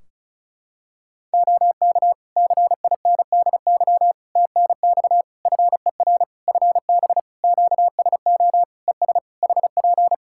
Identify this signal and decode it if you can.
Morse code, heard as a keyed tone, about 36 wpm, 700 Hz. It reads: OK CINDY TNX FER FB QSO ES HP
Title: K1USN SST QSO Practice 35wpm